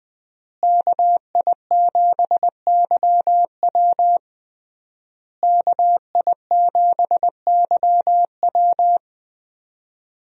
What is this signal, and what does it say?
Morse code, heard as a keyed tone, about 20 wpm, 700 Hz. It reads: KI7YW KI7YW
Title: USA Call Signs - 5 Characters - Repeated 3x - 20wpm